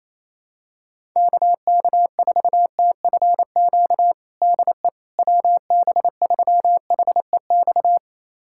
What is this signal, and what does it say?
Morse code, heard as a keyed tone, about 28 wpm, 700 Hz. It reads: KK4TFQ DE WB3HEX
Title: USA Callsign DE Callsign - 6 Characters - 28wpm